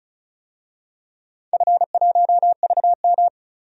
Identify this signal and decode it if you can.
Morse code, heard as a keyed tone, about 36 wpm, 700 Hz. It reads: F1VM
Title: DXCC-100 Call Signs - 4 Characters - 35wpm